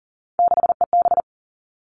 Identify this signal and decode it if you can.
Morse code, heard as a keyed tone, about 42 wpm, 700 Hz. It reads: 6EB
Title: Three Challenging Letter-Number Combinations 40wpm